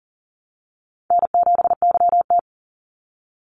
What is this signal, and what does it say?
Morse code, heard as a keyed tone, about 40 wpm, 700 Hz. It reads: N7YT
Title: Call Signs Repeated 3x - 40wpm